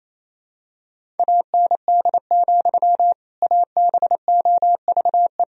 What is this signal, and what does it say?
Morse code, heard as a keyed tone, about 28 wpm, 700 Hz. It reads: AND, ABOVE
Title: Sentences from Top 1000 Words - Review Part 2 - 28wpm